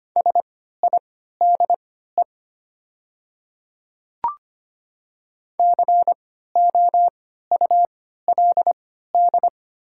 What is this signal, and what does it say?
Morse code, heard as a keyed tone, about 25 wpm, spaced as for 14 wpm, 700 Hz. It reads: SIDE COULD
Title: 3x Character Spacing - Top 100 Words - 25wpm